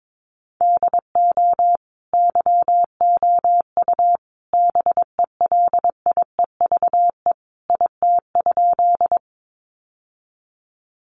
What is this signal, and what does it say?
Morse code, heard as a keyed tone, about 22 wpm, 700 Hz. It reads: DO YOU BELIEVE IT?
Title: Sentences from Top 1000 Words 22wpm